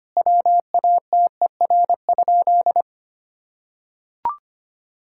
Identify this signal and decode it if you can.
Morse code, heard as a keyed tone, about 25 wpm, 700 Hz. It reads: WATER?
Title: Sentences from Top 1000 Words - Review Part 5 - 25wpm